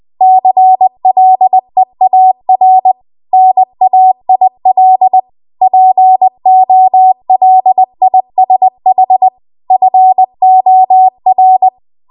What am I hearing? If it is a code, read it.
Morse code, heard as a keyed tone, about 20 wpm, 750 Hz. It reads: CLEAR NAIL POLISH FOR